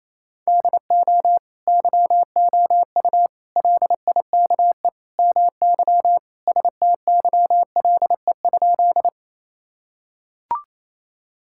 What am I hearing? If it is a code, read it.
Morse code, heard as a keyed tone, about 28 wpm, 700 Hz. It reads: DO YOU LIKE MY STYLE?
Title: Sentences from Top 1000 Words - Review Part 5 - 28wpm